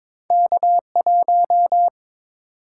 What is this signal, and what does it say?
Morse code, heard as a keyed tone, about 30 wpm, spaced as for 26 wpm, 700 Hz. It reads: K1
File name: Speed-Racing - Two Letter-Number Combinations - 22wpm